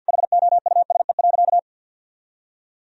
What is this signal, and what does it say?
Morse code, heard as a keyed tone, about 52 wpm, 700 Hz. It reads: HOUSE.
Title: Sentences from Top 1000 Words - Review Part 4 - 50wpm